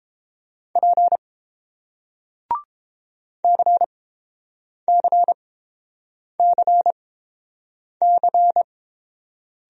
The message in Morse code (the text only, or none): P CCCC